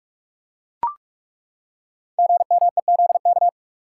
Morse code, heard as a keyed tone, about 43 wpm, 700 Hz. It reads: GMEZK
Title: Code Groups - 5 Letter:Numbers - 45wpm